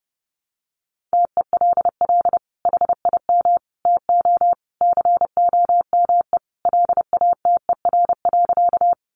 Morse code, heard as a keyed tone, about 30 wpm, 700 Hz. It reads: TELL HIM TO COME LATER.
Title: Sentences from Top 700 Words 30wpm